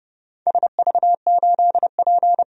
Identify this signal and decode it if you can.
Morse code, heard as a keyed tone, about 30 wpm, 700 Hz. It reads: SV8P
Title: DXCC-100 Call Signs - 4 Characters - 30wpm